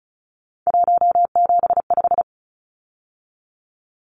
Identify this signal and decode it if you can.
Morse code, heard as a keyed tone, about 34 wpm, 700 Hz. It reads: 175
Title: Numbers - 3-Digit 35wpm